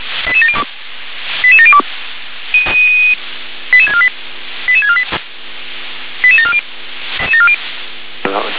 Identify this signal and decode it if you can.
Five-tone selective call (ZVEI), heard as quick runs of five tones, tones of about 70 ms. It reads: 88702 88702 88648 88648 88740
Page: /357/SDR/ZVEI/